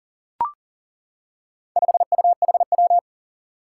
Morse code, heard as a keyed tone, about 42 wpm, 700 Hz. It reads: 5UHW
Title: Four Challenging Letter-Number Combinations 40wpm